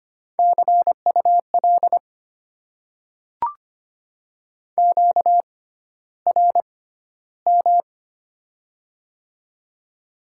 Morse code, heard as a keyed tone, about 25 wpm, spaced as for 13 wpm, 700 Hz. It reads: CUL QRM
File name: Top 90 QSO Elements - 6x Character Spacing - 25wpm